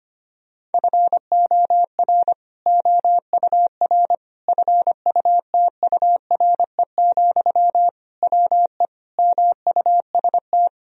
Morse code, heard as a keyed tone, about 25 wpm, 700 Hz. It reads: FOR OUR FUTURE, WE MUST
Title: Sentences from Top 1000 Words - Review Part 3 - 25wpm